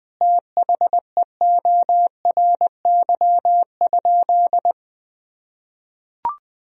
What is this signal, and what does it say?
Morse code, heard as a keyed tone, about 20 wpm, 700 Hz. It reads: THEORY?
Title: Sentences from Top 1000 Words - Review Part 3 - 20wpm